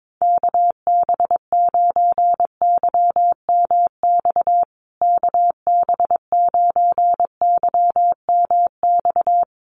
Morse code, heard as a keyed tone, about 22 wpm, 700 Hz. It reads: KB9YMX KB9YMX
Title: USA Call Signs Repeated 3x 22wpm